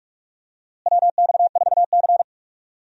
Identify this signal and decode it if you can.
Morse code, heard as a keyed tone, about 42 wpm, 700 Hz. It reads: WX4C
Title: USA Call Signs - 4 Characters - Repeated 2x - 45wpm